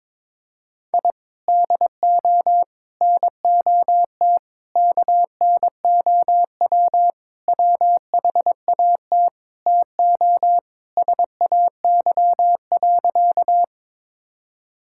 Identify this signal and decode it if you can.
Morse code, heard as a keyed tone, about 22 wpm, 700 Hz. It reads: I DO NOT KNOW WHAT TO SAY.
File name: practice / 22wpm / Sentences with 4-Word Phrases - 22wpm